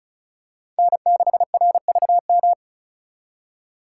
Morse code, heard as a keyed tone, about 36 wpm, 700 Hz. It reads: N6RVM
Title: USA Call Signs - 5 Characters - 4x Character Spacing - 35wpm